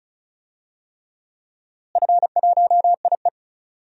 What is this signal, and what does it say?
Morse code, heard as a keyed tone, about 34 wpm, 700 Hz. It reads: F1IE
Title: DXCC-100 Call Signs - 4 Characters - Repeated 3x - 35wpm